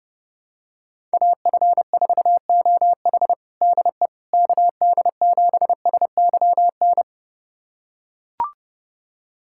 Morse code, heard as a keyed tone, about 30 wpm, 700 Hz. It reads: AF4OH DE KD7SYN
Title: USA Callsign DE Callsign 30wpm